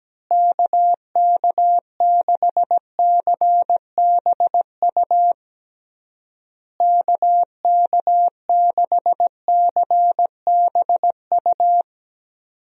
Morse code, heard as a keyed tone, about 17 wpm, 700 Hz. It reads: KK6CBU KK6CBU
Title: USA Call Signs - 6 Characters - Repeated 3x - 17wpm